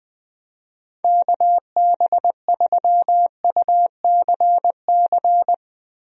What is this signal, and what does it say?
Morse code, heard as a keyed tone, about 20 wpm, 700 Hz. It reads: KB3UCC
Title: USA Call Signs - 6 Characters - Repeated 3x - 20wpm